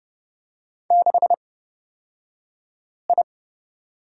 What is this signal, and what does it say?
Morse code, heard as a keyed tone, about 30 wpm, spaced as for 5 wpm, 700 Hz. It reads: 6I
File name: Lesson 37d - Letter x - Two Characters 30wpm